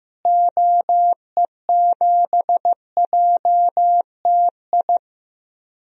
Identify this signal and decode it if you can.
Morse code, heard as a keyed tone, about 15 wpm, 700 Hz. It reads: OE7JTI
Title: DXCC-100 Call Signs - 6 Characters - Repeated 3x - 15wpm